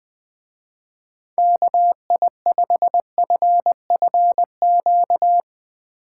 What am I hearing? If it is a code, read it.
Morse code, heard as a keyed tone, about 20 wpm, 700 Hz. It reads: KI5FFQ
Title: Speed Racing - USA 2x3 Call Signs 20wpm